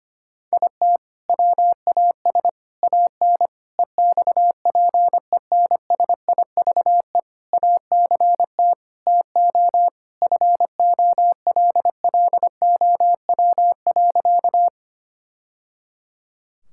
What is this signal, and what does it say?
Morse code, heard as a keyed tone, about 25 wpm, 700 Hz. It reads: IT WAS AN EXPENSIVE ACT TO FOLLOW.
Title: Sentences from Top 1000 Words - Review Part 6 - 25wpm